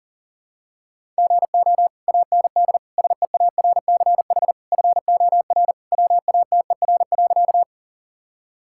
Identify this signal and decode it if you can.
Morse code, heard as a keyed tone, about 40 wpm, 700 Hz. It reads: GO AND SEARCH FOR WATER.
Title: Sentences from Top 1000 Words - Review Part 5 - 40wpm